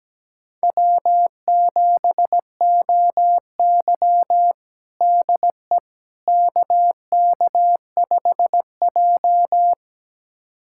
Morse code, heard as a keyed tone, about 17 wpm, 700 Hz. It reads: W7OY DE KK5J
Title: USA Callsign DE Callsign - 4 Characters - 17wpm